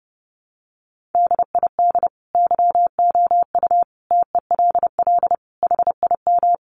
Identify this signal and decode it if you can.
Morse code, heard as a keyed tone, about 28 wpm, 700 Hz. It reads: DID YOU TELL HIM
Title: Sentences from Top 400 Words 30wpm